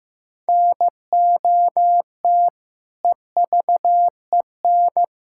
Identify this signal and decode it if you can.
Morse code, heard as a keyed tone, about 15 wpm, 700 Hz. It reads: NOT EVEN
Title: Sentences from Top 1000 Words - Review Part 6 - 15wpm